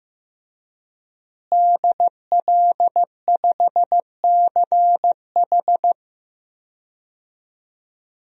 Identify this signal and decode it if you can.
Morse code, heard as a keyed tone, about 15 wpm, 700 Hz. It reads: DL5CH
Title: DXCC-100 Call Signs - 5 Characters - Repeated 3x - 15wpm